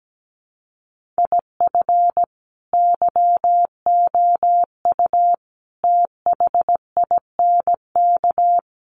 Morse code, heard as a keyed tone, about 17 wpm, 700 Hz. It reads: IF YOU THINK
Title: Sets of 5 Words Encore2 17wpm